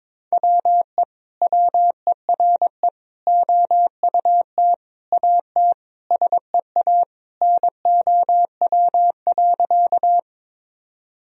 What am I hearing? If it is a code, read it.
Morse code, heard as a keyed tone, about 22 wpm, 700 Hz. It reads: WE WERE OUT AT SEA NOW.
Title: Sentences from Top 1000 Words - Review Part 1 - 22wpm